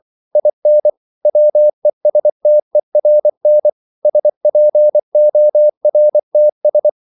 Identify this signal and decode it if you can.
Morse code, heard as a keyed tone, about 24 wpm, 600 Hz. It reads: IN WESTERN SPORTS